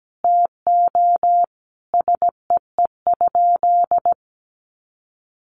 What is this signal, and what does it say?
Morse code, heard as a keyed tone, about 17 wpm, 700 Hz. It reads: TO SEE?
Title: Sentences from Top 100 Words 17wpm